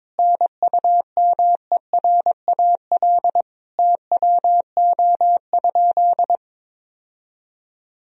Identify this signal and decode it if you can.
Morse code, heard as a keyed tone, about 22 wpm, 700 Hz. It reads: NUMERAL TWO?
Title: Sentences from Top 1000 Words - Review Part 2 - 22wpm